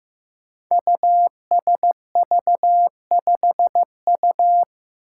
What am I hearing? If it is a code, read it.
Morse code, heard as a keyed tone, about 15 wpm, 700 Hz. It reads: USV5U
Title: Five Challenging Letter-Number Combinations 15wpm